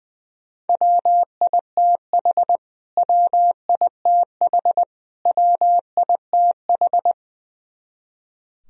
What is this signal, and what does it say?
Morse code, heard as a keyed tone, about 21 wpm, 700 Hz. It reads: WITH WITH WITH
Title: Top 100 Words Repeated 20wpm